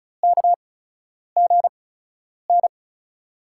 Morse code, heard as a keyed tone, about 36 wpm, spaced as for 10 wpm, 700 Hz. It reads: KGN
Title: USA Call Signs - 6 Characters - 8x Character Spacing - 35wpm